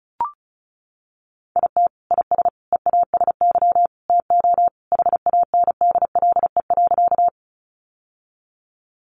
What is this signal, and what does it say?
Morse code, heard as a keyed tone, about 34 wpm, 700 Hz. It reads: IT IS EASY TO HANDLE.
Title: Sentences from Top 1100 Words 35wpm